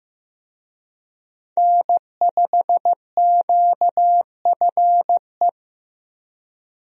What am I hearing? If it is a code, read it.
Morse code, heard as a keyed tone, about 15 wpm, 700 Hz. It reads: N5QFE
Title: USA Call Signs - 5 Characters - Repeated 3x - 15wpm